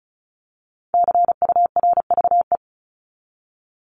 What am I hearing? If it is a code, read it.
Morse code, heard as a keyed tone, about 34 wpm, 700 Hz. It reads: CURVE
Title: Top 1501-1750 Words 35wpm